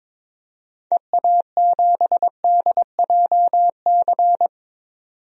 Morse code, heard as a keyed tone, about 22 wpm, 700 Hz. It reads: EA7DJC
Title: DXCC-100 Call Signs - 6 Characters - 22wpm